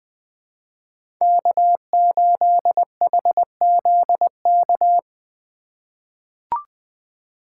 Morse code, heard as a keyed tone, about 21 wpm, 700 Hz. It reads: K8HZK
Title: USA Call Signs - 5 Characters - 2x Character Spacing - 20wpm